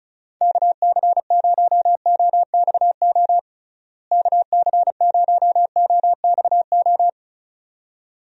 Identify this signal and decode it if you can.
Morse code, heard as a keyed tone, about 36 wpm, 700 Hz. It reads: KC0OXO KC0OXO
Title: USA Call Signs - 6 Characters - Repeated 2x - 35wpm